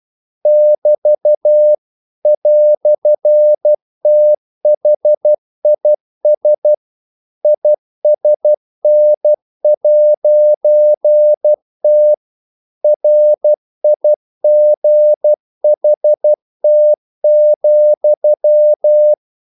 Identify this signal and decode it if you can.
Morse code, heard as a keyed tone, about 12 wpm, 600 Hz. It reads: <BT> "THIS ISN'T RIGHT,
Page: English quotations in CW (telegraphy - morse)